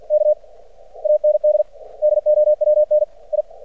Morse code, heard as a keyed tone, about 50 wpm, 600 Hz. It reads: K AND DOWN I
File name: archived / CRSnet